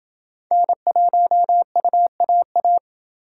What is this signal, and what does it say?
Morse code, heard as a keyed tone, about 29 wpm, 700 Hz. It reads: N1UAA